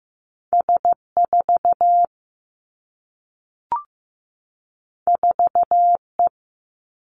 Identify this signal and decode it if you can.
Morse code, heard as a keyed tone, about 15 wpm, 700 Hz. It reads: S 4 4 E
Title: Two Challenging Letter-Number Combinations 15wpm